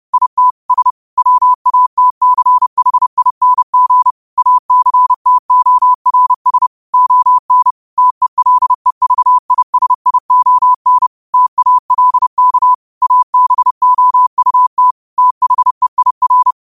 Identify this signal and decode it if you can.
Morse code, heard as a keyed tone, about 30 wpm, 1000 Hz. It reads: IT S WATCHING ACTORS ON TELEVISION TALK ABOUT THEIR
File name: qotdfiles / 30wpm